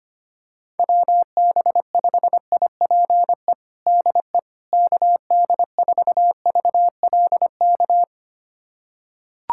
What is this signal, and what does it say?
Morse code, heard as a keyed tone, about 25 wpm, 700 Hz. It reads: WB5IPE DE KD4VLK
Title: USA Callsign DE Callsign 25wpm